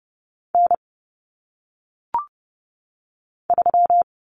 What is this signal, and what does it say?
Morse code, heard as a keyed tone, about 30 wpm, 700 Hz. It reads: N 3
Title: Common Cut Numbers - 1-Digit 30wpm